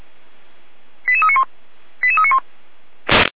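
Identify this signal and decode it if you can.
Five-tone selective call (ZVEI), heard as quick runs of five tones, tones of about 70 ms. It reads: 88391 88391
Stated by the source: Signal received via SDR, ZVEI five-tone.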